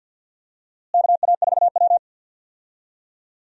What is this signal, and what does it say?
Morse code, heard as a keyed tone, about 53 wpm, 700 Hz. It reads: KA4W
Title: USA Call Signs - 4 Characters - Repeated 3x - 50wpm